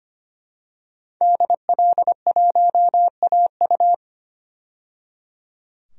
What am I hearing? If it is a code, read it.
Morse code, heard as a keyed tone, about 25 wpm, 700 Hz. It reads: DL1AU
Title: DXCC-100 Call Signs - 5 Characters - 25wpm